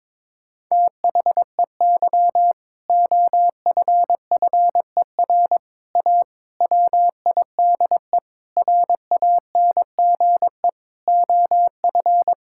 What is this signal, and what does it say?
Morse code, heard as a keyed tone, about 22 wpm, 700 Hz. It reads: THEY OFFER A WIDE RANGE OF
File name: practice / 22wpm / Sentences with 4-Word Phrases - 22wpm